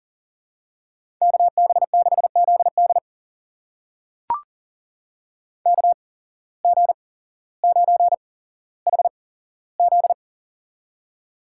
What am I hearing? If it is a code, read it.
Morse code, heard as a keyed tone, about 42 wpm, spaced as for 17 wpm, 700 Hz. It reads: KB6ZD KG9HZ
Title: USA Call Signs - 5 Characters - 8x Character Spacing - 40wpm